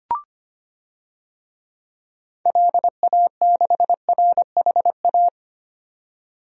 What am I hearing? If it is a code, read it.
Morse code, heard as a keyed tone, about 25 wpm, 700 Hz. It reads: LA6RHA
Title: DXCC-100 Call Signs - 6 Characters - 25wpm